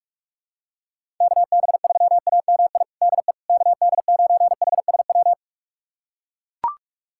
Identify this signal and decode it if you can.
Morse code, heard as a keyed tone, about 41 wpm, 700 Hz. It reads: KB3AMI DE KD9HSW
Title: USA Callsign DE Callsign - 6 Characters - 45wpm